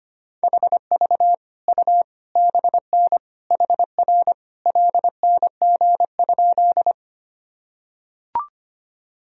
Morse code, heard as a keyed tone, about 25 wpm, 700 Hz. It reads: HV U BN HR LNG?
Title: Sentences from Top 200 Words and Abbreviations 25wpm